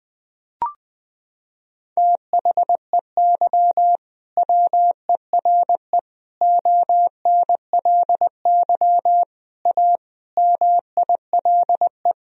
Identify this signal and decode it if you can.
Morse code, heard as a keyed tone, about 21 wpm, 700 Hz. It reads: THEY WERE ONLY A MILE
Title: Sentences from Top 1000 Words - Review Part 1 - 20wpm